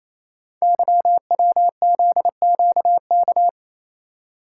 Morse code, heard as a keyed tone, about 27 wpm, 700 Hz. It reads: YWZQK
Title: Code Groups - 5 Letters - 28wpm